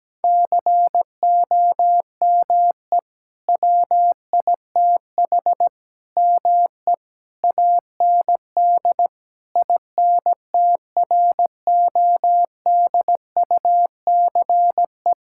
Morse code, heard as a keyed tone, about 17 wpm, 700 Hz. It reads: COME WITH ME AND INTRODUCE
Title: Sentences from Top 1600 Words 17wpm